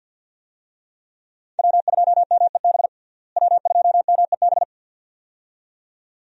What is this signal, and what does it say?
Morse code, heard as a keyed tone, about 47 wpm, 700 Hz. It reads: W2MEB W2MEB
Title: USA Call Signs - 5 Characters - Repeated 2x - 50wpm